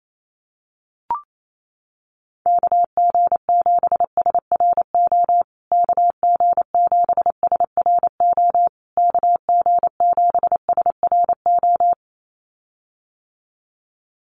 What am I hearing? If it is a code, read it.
Morse code, heard as a keyed tone, about 28 wpm, 700 Hz. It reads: KG7SRO KG7SRO KG7SRO
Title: USA Call Signs Repeated 3x 28wpm